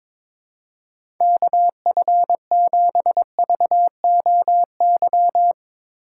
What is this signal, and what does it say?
Morse code, heard as a keyed tone, about 22 wpm, 700 Hz. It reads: KF7VOY
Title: USA Call Signs - 4x Character Spacing - 22wpm